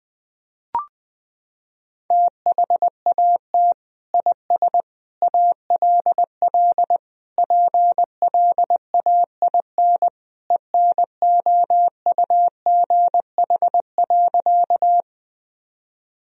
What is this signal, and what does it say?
Morse code, heard as a keyed tone, about 20 wpm, 700 Hz. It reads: THAT IS ALL PLAIN ENOUGH.
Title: Sentences from Top 1000 Words - Review Part 2 - 20wpm